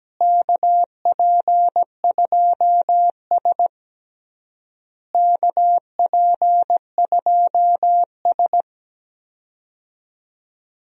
Morse code, heard as a keyed tone, about 17 wpm, 700 Hz. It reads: KP2S KP2S
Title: DXCC-100 Call Signs - 4 Characters - Repeated 3x - 17wpm